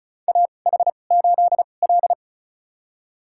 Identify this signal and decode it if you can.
Morse code, heard as a keyed tone, about 36 wpm, spaced as for 25 wpm, 700 Hz. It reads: AH8L
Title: USA Call Signs - 4 Characters - 2x Character Spacing - 35wpm